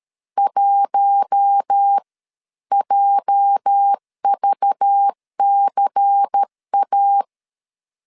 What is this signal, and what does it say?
Morse code, heard as a keyed tone, about 13 wpm, spaced as for 11 wpm, 800 Hz. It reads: 1JVCA
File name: LEZIONI / 8 YJ45 / 8 YJ 12wpm 2235